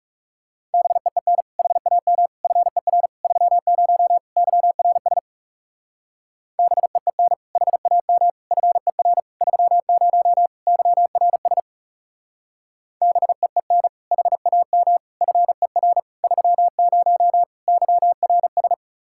Morse code, heard as a keyed tone, about 37 wpm, 700 Hz. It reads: BEEN HAM FER 30 YRS BEEN HAM FER 30 YRS BEEN HAM FER 30 YRS
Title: Speed-Racing Common QSO Phrases Encore - 30wpm